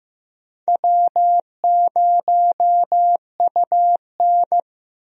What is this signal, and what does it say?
Morse code, heard as a keyed tone, about 15 wpm, 700 Hz. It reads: W0UN
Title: USA Call Signs - 4 Characters - 15wpm